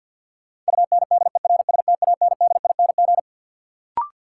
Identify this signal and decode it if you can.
Morse code, heard as a keyed tone, about 53 wpm, 700 Hz. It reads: UNDERSTANDING
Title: Sentences from Top 1000 Words - Review Part 3 - New Word Drill - 50wpm